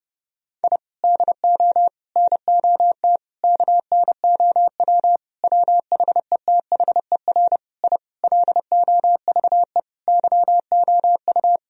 Morse code, heard as a keyed tone, about 30 wpm, 700 Hz. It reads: I DO NOT KNOW WHETHER I LOVE YOU
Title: Sentences from Top 1000 Words - Review Part 3 - 30wpm